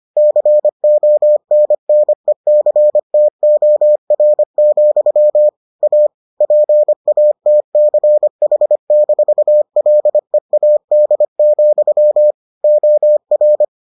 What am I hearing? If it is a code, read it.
Morse code, heard as a keyed tone, about 25 wpm, 600 Hz. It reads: CONNECTOR, A PATCH-LEAD, OR